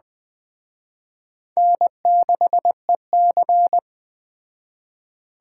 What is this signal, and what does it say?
Morse code, heard as a keyed tone, about 20 wpm, 700 Hz. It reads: N6EC
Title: Speed Racing - USA 1x2 Call Signs 17wpm